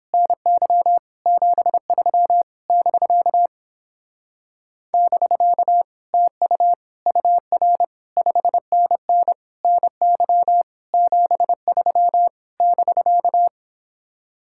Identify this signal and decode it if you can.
Morse code, heard as a keyed tone, about 29 wpm, 700 Hz. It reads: NY 73 <BK> <BK> TU UR 5NN NY 73 <BK>